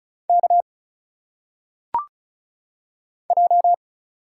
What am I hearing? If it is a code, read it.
Morse code, heard as a keyed tone, about 36 wpm, 700 Hz. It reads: K J
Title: Single Letters - Phonetic Pronunciation 35wpm